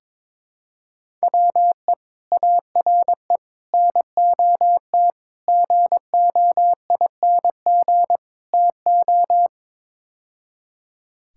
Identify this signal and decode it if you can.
Morse code, heard as a keyed tone, about 22 wpm, 700 Hz. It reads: WE ARE NOT GOING TO
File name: practice / 22wpm / Build A Phrase - 5-Word Phrases - 22wpm